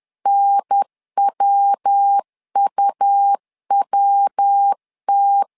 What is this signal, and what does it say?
Morse code, heard as a keyed tone, about 10 wpm, 800 Hz. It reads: NWUWT
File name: LEZIONI / 3 MWRO / 3 MW 10wpm 1959